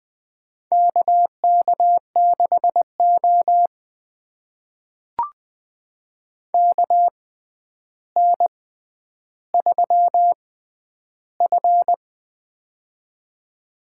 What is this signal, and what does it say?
Morse code, heard as a keyed tone, about 20 wpm, spaced as for 10 wpm, 700 Hz. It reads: KK6O KN3F
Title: USA Call Signs - 4 Characters - 6x Character Spacing - 20wpm